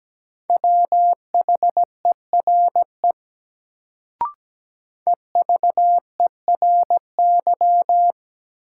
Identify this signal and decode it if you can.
Morse code, heard as a keyed tone, about 17 wpm, 700 Hz. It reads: WHERE EVERY
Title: Sentences from Top 200 Words - New Word Drill - 17wpm